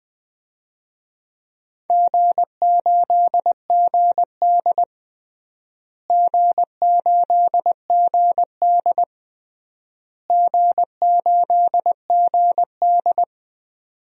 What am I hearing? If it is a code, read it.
Morse code, heard as a keyed tone, about 20 wpm, 700 Hz. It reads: G8GD G8GD G8GD
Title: DXCC-100 Call Signs - 4 Characters - Repeated 3x - 20wpm